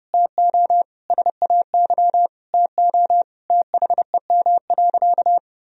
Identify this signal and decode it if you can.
Morse code, heard as a keyed tone, about 30 wpm, 700 Hz. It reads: TO SAY TO THEM.
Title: Sentences from Top 1000 Words - Review Part 1 - 30wpm